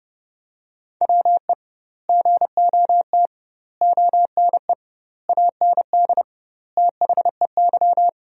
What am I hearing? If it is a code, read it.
Morse code, heard as a keyed tone, about 30 wpm, 700 Hz. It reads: WE GOT ONE AND THEY
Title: 2x Word-Spacing - Sentences from Top 300 Words - 30wpm